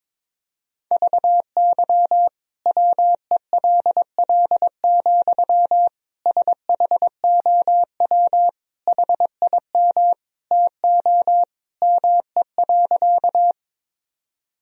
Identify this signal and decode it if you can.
Morse code, heard as a keyed tone, about 22 wpm, 700 Hz. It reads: VY WELL, SHOW HIM TO ME.
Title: Sentences from Top 200 Words and Abbreviations 22wpm